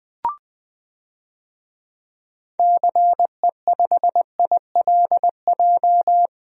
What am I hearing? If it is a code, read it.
Morse code, heard as a keyed tone, about 20 wpm, 700 Hz. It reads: CE5ILJ
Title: DXCC-100 Call Signs - 6 Characters - 20wpm